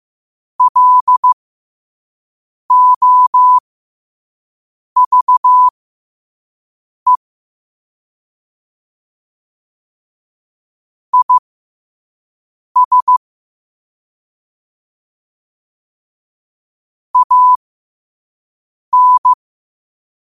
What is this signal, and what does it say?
Morse code, heard as a keyed tone, about 15 wpm, spaced as for 5 wpm, 1000 Hz. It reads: LOVE IS AN
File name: qotdfiles / 5wpm